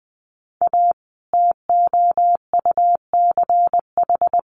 Morse code, heard as a keyed tone, about 19 wpm, 700 Hz. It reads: A TOUCH
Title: A Sound Of Thunder 20wpm